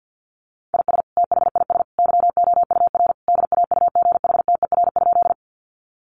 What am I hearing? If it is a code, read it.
Morse code, heard as a keyed tone, about 50 wpm, 700 Hz. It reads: IS THIS YOUR DAUGHTER?
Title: Sentences from Top 1500 Words 50wpm